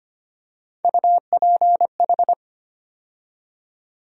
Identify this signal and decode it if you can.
Morse code, heard as a keyed tone, about 25 wpm, 700 Hz. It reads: UPH
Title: USA Call Sign Postfixes - 3 Letters 25wpm